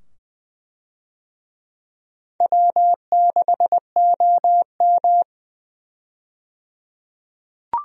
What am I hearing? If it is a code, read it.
Morse code, heard as a keyed tone, about 20 wpm, 700 Hz. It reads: W6OM
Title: USA Call Signs - 4 Characters - 20wpm